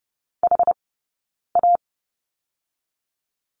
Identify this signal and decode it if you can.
Morse code, heard as a keyed tone, about 30 wpm, spaced as for 9 wpm, 690 Hz. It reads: HA
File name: Lesson 19d - Letter w - Two Characters 30wpm